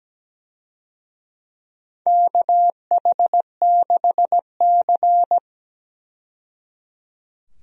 Morse code, heard as a keyed tone, about 17 wpm, 700 Hz. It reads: KH6C
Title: DXCC-100 Call Signs - 4 Characters - 17wpm